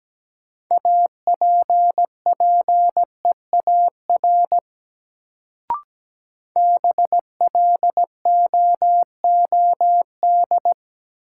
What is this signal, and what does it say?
Morse code, heard as a keyed tone, about 17 wpm, 700 Hz. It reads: APPEAR BLOOD
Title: Sentences from Top 1000 Words - Review Part 2 - New Word Drill - 17wpm